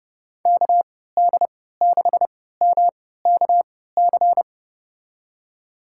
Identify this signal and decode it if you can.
Morse code, heard as a keyed tone, about 30 wpm, spaced as for 17 wpm, 700 Hz. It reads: KD6MKC
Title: USA Call Signs - 6 Characters - 3x Character Spacing - 30wpm